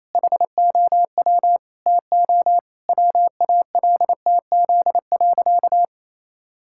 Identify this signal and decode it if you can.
Morse code, heard as a keyed tone, about 28 wpm, 700 Hz. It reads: HOW TO WALTZ.
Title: Pangram Challenge v2 - 28wpm